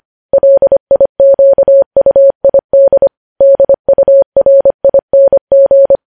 Morse code, heard as a keyed tone, about 25 wpm, 550 Hz. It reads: LIQUID DURING
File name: podcast / News headlines-1x-25x25